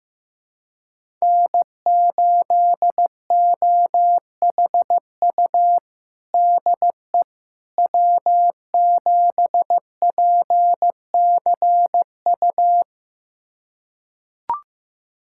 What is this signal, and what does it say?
Morse code, heard as a keyed tone, about 15 wpm, 700 Hz. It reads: N8OHU DE W7PCU
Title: USA Callsign DE Callsign - 5 Characters - 15wpm